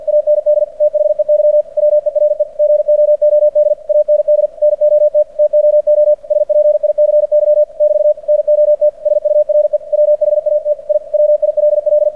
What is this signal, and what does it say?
Morse code, heard as a keyed tone, about 46 wpm, 600 Hz. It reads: AND THEY WERE GOOD AND NOT TOO SPICY <BT> NOT SURE WHAT I WILL
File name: archived / CRSnet